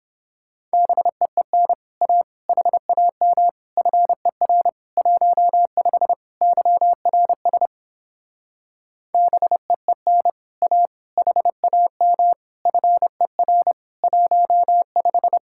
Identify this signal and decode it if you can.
Morse code, heard as a keyed tone, about 29 wpm, 700 Hz. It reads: BEEN A HAM FER 15 YRS BEEN A HAM FER 15